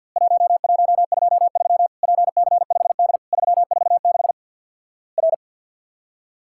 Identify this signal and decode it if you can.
Morse code, heard as a keyed tone, about 50 wpm, 700 Hz. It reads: 1123 PC5D 346 R
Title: Worked All Europe DX Contest - QTC Exchange with Repeats - 50wpm